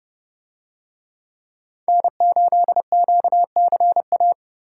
Morse code, heard as a keyed tone, about 29 wpm, 700 Hz. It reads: N8QCA